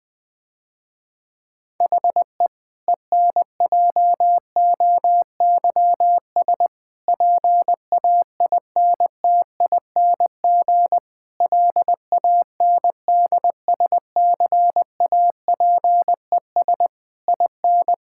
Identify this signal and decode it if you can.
Morse code, heard as a keyed tone, about 20 wpm, 700 Hz. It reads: HE ENJOYS PAINTING LANDSCAPES IN
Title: Difficult Sentences with Hints - 20wpm